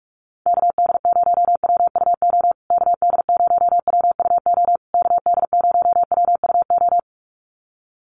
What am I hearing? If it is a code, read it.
Morse code, heard as a keyed tone, about 44 wpm, 700 Hz. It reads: KD0WUO KD0WUO KD0WUO
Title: USA Call Signs Repeated 3x 45wpm